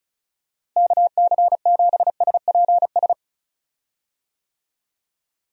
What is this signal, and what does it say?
Morse code, heard as a keyed tone, about 34 wpm, 700 Hz. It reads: KC7SPS
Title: USA Call Signs - 6 Characters - Repeated 3x - 35wpm